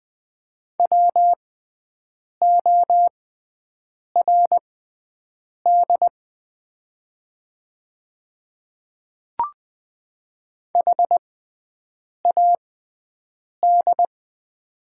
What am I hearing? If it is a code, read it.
Morse code, heard as a keyed tone, about 20 wpm, spaced as for 7 wpm, 700 Hz. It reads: WORD HAD
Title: 6x Character Spacing - Top 100 Words - 20wpm